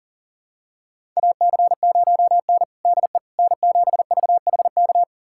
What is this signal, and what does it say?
Morse code, heard as a keyed tone, about 40 wpm, 700 Hz. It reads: AC0N DE N7VHK
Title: USA Callsign DE Callsign 40wpm